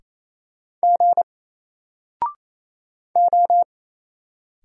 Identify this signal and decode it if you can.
Morse code, heard as a keyed tone, about 28 wpm, 700 Hz. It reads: G O
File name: Single-Letter-Number-28wpm